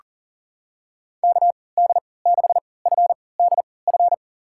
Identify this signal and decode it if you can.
Morse code, heard as a keyed tone, about 38 wpm, spaced as for 22 wpm, 700 Hz. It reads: KD6FDF
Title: USA Call Signs - 6 Characters - 3x Character Spacing - 40wpm